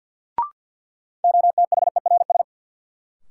Morse code, heard as a keyed tone, about 47 wpm, 700 Hz. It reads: OTHERS
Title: Sentences from Top 1500 Words - New Word Drill Alt - 50wpm